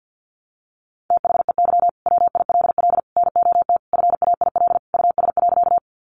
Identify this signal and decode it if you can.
Morse code, heard as a keyed tone, about 48 wpm, 700 Hz. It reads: THEY WILL NOT FAIL US.
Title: Sentences from Top 1100 Words 50wpm